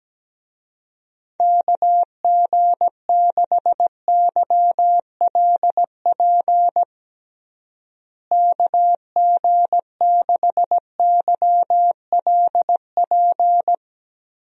Morse code, heard as a keyed tone, about 17 wpm, 700 Hz. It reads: KG6YLP KG6YLP
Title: USA Call Signs - 6 Characters - Repeated 3x - 17wpm